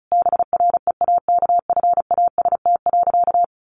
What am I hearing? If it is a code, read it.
Morse code, heard as a keyed tone, about 34 wpm, 700 Hz. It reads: BREAKFAST.
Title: Sentences from Top 1500 Words 35wpm